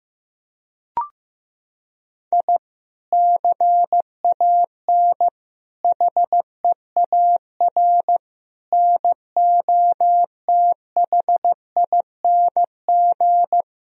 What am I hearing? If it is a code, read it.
Morse code, heard as a keyed tone, about 16 wpm, 700 Hz. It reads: I CAN HEAR NOTHING
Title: Sentences from Top 1000 Words - Review Part 4 - 15wpm